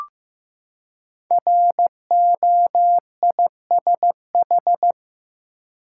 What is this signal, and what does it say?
Morse code, heard as a keyed tone, about 15 wpm, 700 Hz. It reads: ROISH
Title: Code Groups - 5 Letters - 15wpm